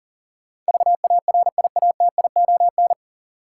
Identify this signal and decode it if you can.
Morse code, heard as a keyed tone, about 40 wpm, 700 Hz. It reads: VARIATION
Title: Sentences from Top 1600 Words - New Word Drill - 40wpm